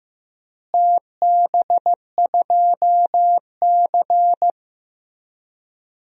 Morse code, heard as a keyed tone, about 15 wpm, 700 Hz. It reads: TB2C
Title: DXCC-100 Call Signs - 4 Characters - Repeated 2x - 15wpm